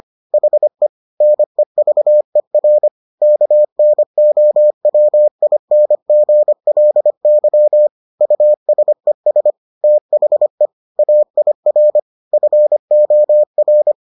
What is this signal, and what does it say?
Morse code, heard as a keyed tone, about 25 wpm, 600 Hz. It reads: HE NEVER KNOWINGLY USES THE AIR FOR